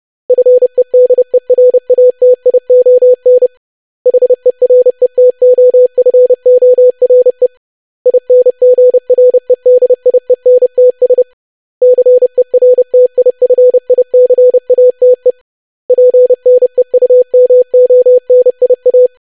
Morse code, heard as a keyed tone, about 30 wpm, 500 Hz. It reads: FEDERATION HERETOFORE INGREDIENTS CERTIFICATE PNEUMONIA
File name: lesson / web / longwords-25wpm